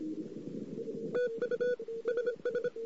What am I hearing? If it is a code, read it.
Morse code, heard as a keyed tone, about 26 wpm, 490 Hz. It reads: TU SS
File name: CW / 2008-ss-cw-k5tr-op-n5rz -14:00